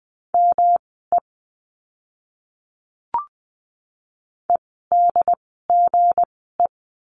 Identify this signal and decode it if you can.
Morse code, heard as a keyed tone, about 20 wpm, spaced as for 15 wpm, 700 Hz. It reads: ME EDGE
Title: 160 Lesson 26f - Letter g - Words 20wpm